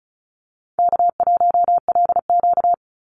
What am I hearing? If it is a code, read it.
Morse code, heard as a keyed tone, about 34 wpm, 700 Hz. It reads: K1LQ
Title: USA Call Signs 35wpm